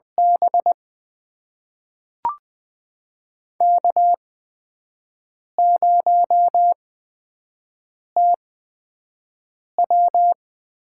Morse code, heard as a keyed tone, about 20 wpm, spaced as for 5 wpm, 700 Hz. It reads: B K0TW
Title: USA Call Signs - 4 Characters - 8x Character Spacing - 20wpm